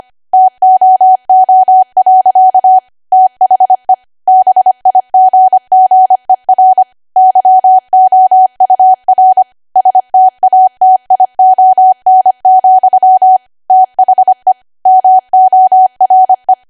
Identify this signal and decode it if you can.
Morse code, heard as a keyed tone, about 25 wpm, 750 Hz. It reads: TOO. THE BIGGER YOUR STATION, THE MORE
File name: Archive / 25 WPM / 25WPM